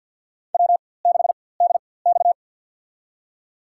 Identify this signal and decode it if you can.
Morse code, heard as a keyed tone, about 47 wpm, spaced as for 23 wpm, 700 Hz. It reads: W6DX
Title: USA Call Signs - 4x Character Spacing - 50wpm